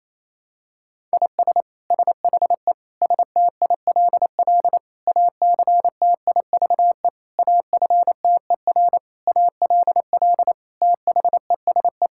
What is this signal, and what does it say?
Morse code, heard as a keyed tone, about 28 wpm, 700 Hz. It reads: IS SHE STILL ACTIVE AFTER ALL THESE
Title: Sentences from Top 1000 Words - Review Part 6 - 28wpm